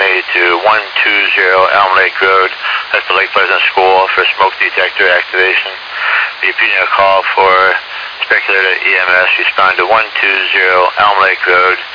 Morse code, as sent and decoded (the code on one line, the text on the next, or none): none